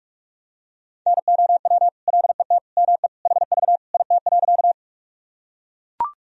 Morse code, heard as a keyed tone, about 41 wpm, 700 Hz. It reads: NOW LET ME HV IT.
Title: Sentences from Top 300 Words and Abbreviations 45wpm